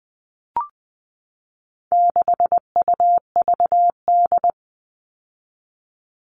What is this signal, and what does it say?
Morse code, heard as a keyed tone, about 20 wpm, 700 Hz. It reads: 6UVD
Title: All Cut Numbers - 4-Digit 20wpm